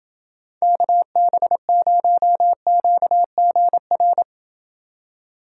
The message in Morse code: KB0QGR

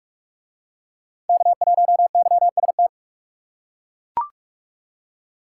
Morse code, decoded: K1YST

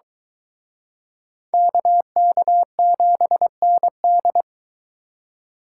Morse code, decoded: KK7ND

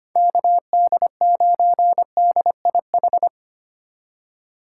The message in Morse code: KD9DIH

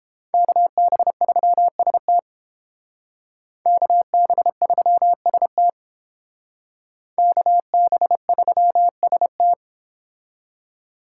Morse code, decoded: KB3ST KB3ST KB3ST

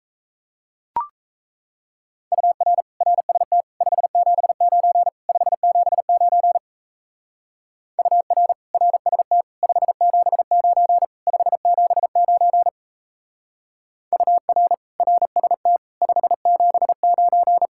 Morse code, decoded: UR RST 579 579 UR RST 579 579 UR RST 579

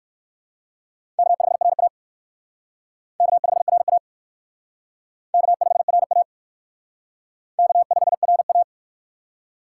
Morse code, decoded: K5RA K5RA K5RA K5RA

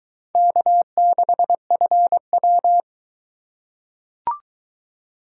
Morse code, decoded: K6FW

